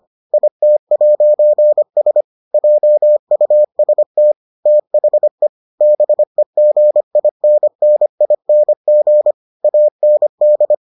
Morse code, heard as a keyed tone, about 25 wpm, 600 Hz. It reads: IT'S JUST THE BEGINNING AND